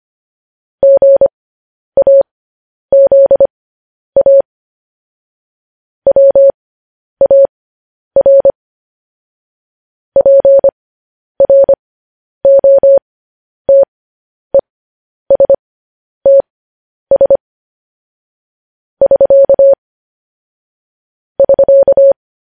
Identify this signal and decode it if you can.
Morse code, heard as a keyed tone, about 25 wpm, spaced as for 10 wpm, 550 Hz. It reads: GAZA WAR PROTESTS <SK> <SK>